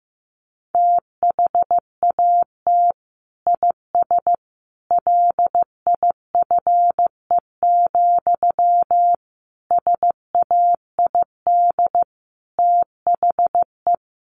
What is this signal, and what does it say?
Morse code, heard as a keyed tone, about 15 wpm, 700 Hz. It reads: THAT IS LIFE, SAID THE
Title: Sentences from Top 900 Words 15wpm